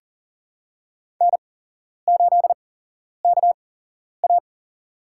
Morse code, heard as a keyed tone, about 40 wpm, spaced as for 11 wpm, 700 Hz. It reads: N8KA
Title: USA Call Signs - 4 Characters - 8x Character Spacing - 40wpm